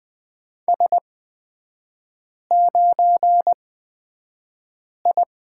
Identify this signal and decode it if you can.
Morse code, heard as a keyed tone, about 20 wpm, spaced as for 5 wpm, 700 Hz. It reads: S9I
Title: 224 Lesson 36e - BT - Three Characters 20wpm